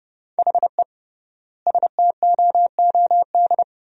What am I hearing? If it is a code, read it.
Morse code, heard as a keyed tone, about 30 wpm, 700 Hz. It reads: HE STOOD
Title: 3x Word-Spacing - Sentences from Top 500 Words - 30wpm